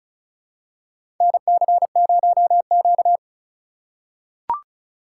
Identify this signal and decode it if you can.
Morse code, heard as a keyed tone, about 34 wpm, 700 Hz. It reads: NC0Q
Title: USA Call Signs - 4 Characters - 3x Character Spacing - 35wpm